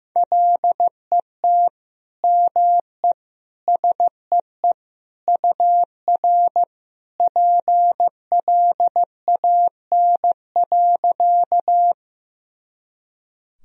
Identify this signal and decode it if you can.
Morse code, heard as a keyed tone, about 16 wpm, 700 Hz. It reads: LET ME SEE UR PLAN.
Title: Sentences from Top 500 Words and Abbreviations 15wpm